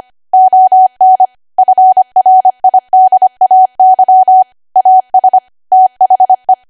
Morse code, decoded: ON FRIDAY AS THE